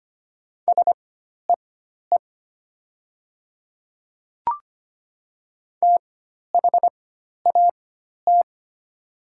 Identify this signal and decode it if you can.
Morse code, heard as a keyed tone, about 25 wpm, spaced as for 12 wpm, 700 Hz. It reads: SEE THAT
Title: 4x Character Spacing - Top 100 Words - 25wpm